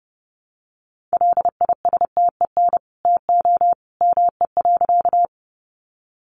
Morse code, heard as a keyed tone, about 30 wpm, 700 Hz. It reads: LISTEN TO ME.
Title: Sentences from Top 400 Words 30wpm